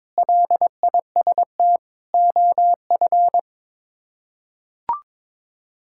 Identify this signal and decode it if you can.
Morse code, heard as a keyed tone, about 22 wpm, 700 Hz. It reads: LIST OF
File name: practice / 22wpm / Sets of 7 Words - 22wpm